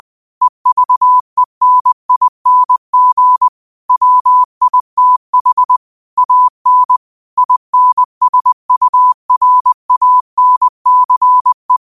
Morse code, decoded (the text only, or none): EVENING WITH AN INSURANCE